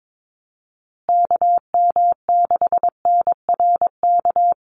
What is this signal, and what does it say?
Morse code, heard as a keyed tone, about 22 wpm, 700 Hz. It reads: KM6NRK
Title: USA Call Signs Repeated 2x 22wpm